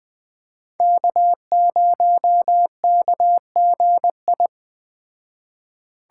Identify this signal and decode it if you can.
Morse code, heard as a keyed tone, about 21 wpm, 700 Hz. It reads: K0KGI
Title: Speed Racing - USA 1x3 Call Signs 17wpm